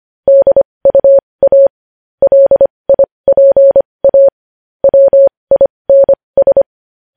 Morse code, heard as a keyed tone, about 25 wpm, spaced as for 20 wpm, 550 Hz. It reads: DUA LIPA WINS
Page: news-headlines-in-morse-code-at-20-wpm